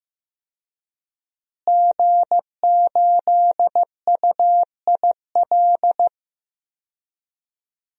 Morse code, heard as a keyed tone, about 15 wpm, 700 Hz. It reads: G8UIL
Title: DXCC-100 Call Signs - 5 Characters - Repeated 3x - 15wpm